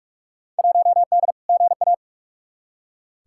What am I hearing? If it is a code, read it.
Morse code, heard as a keyed tone, about 47 wpm, 700 Hz. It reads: 1D GA
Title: Field Day Exchanges - 45wpm